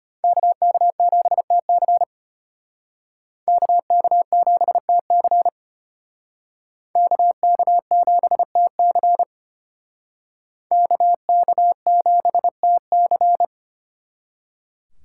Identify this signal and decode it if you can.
Morse code, heard as a keyed tone, about 32 wpm, 700 Hz. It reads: KK7TC KK7TC KK7TC KK7TC